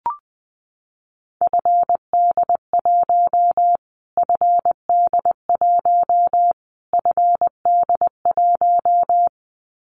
Morse code, heard as a keyed tone, about 20 wpm, 700 Hz. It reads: FD1 FD1 FD1
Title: Callsign Prefixes for Top 25 DXCC - Repeated 3x - 20wpm